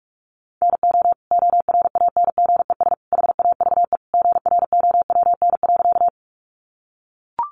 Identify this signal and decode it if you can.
Morse code, heard as a keyed tone, about 44 wpm, 700 Hz. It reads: NO ORANGES HAVE GROWN.
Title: Easy Phrases 45wpm